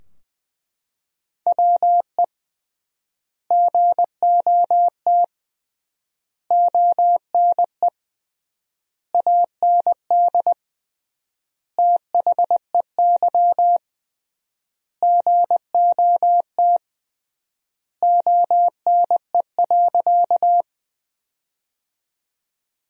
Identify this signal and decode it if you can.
Morse code, heard as a keyed tone, about 21 wpm, 700 Hz. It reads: WE GOT ONE AND THEY GOT ONE.
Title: 3x Word-Spacing - Sentences from Top 300 Words - 20wpm